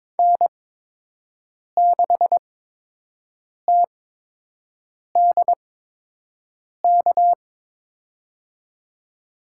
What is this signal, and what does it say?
Morse code, heard as a keyed tone, about 22 wpm, spaced as for 6 wpm, 700 Hz. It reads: N6TDK
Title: USA Call Signs - 5 Characters - 8x Character Spacing - 22wpm